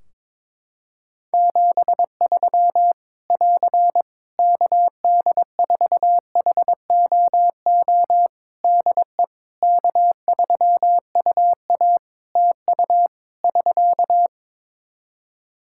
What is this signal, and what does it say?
Morse code, heard as a keyed tone, about 22 wpm, 700 Hz. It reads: 73 <AR> KD4HOO DE K3UA TU <SK>
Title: Practice QSO - Exchange 4 - Ending - 22wpm